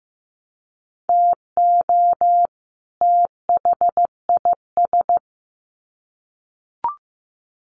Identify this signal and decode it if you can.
Morse code, heard as a keyed tone, about 14 wpm, 700 Hz. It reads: TO THIS
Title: Sets of 2 Words 15wpm